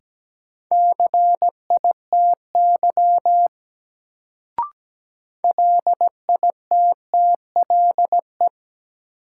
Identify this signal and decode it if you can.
Morse code, heard as a keyed tone, about 17 wpm, 700 Hz. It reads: CITY LITTLE
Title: Sets of 2 Words - New Word Drill - 17wpm